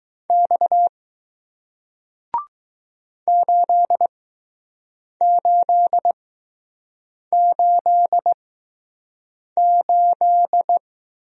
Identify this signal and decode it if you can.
Morse code, heard as a keyed tone, about 20 wpm, spaced as for 6 wpm, 700 Hz. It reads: X 8888